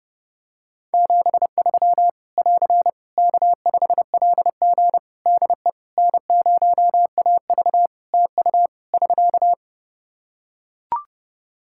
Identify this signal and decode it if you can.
Morse code, heard as a keyed tone, about 30 wpm, 700 Hz. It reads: 73 <AR> K5LG DE N0AV TU <SK>
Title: K1USN SST QSO Practice 30wpm